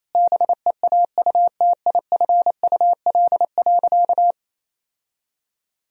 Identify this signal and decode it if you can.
Morse code, heard as a keyed tone, about 28 wpm, 700 Hz. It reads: BEAUTIFUL.
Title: Sentences from Top 1000 Words - Review Part 4 - 28wpm